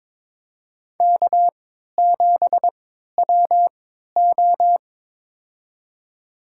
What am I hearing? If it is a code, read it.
Morse code, heard as a keyed tone, about 22 wpm, spaced as for 13 wpm, 700 Hz. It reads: K7WO
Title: USA Call Signs - 4 Characters - 3x Character Spacing - 22wpm